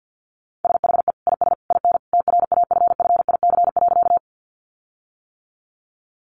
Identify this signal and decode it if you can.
Morse code, heard as a keyed tone, about 50 wpm, 700 Hz. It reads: SHE IS IN TRAFFIC.
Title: Sentences from Top 1300 Words 50wpm